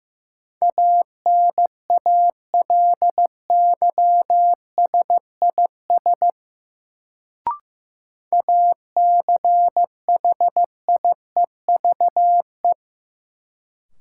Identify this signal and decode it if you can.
Morse code, heard as a keyed tone, about 15 wpm, 700 Hz. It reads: ANALYSIS ACHIEVE
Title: Sentences from Top 1000 Words - Review Part 5 - New Word Drill - 15wpm